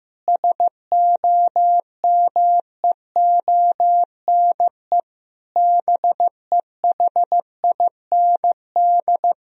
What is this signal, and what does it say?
Morse code, heard as a keyed tone, about 15 wpm, 700 Hz. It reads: SOMEONE BEHIND
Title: Sentences from Top 1000 Words - Review Part 4 - 15wpm